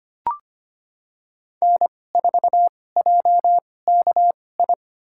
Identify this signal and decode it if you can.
Morse code, heard as a keyed tone, about 25 wpm, spaced as for 18 wpm, 700 Hz. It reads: N4JKI
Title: USA Call Signs - 5 Characters - 2x Character Spacing - 25wpm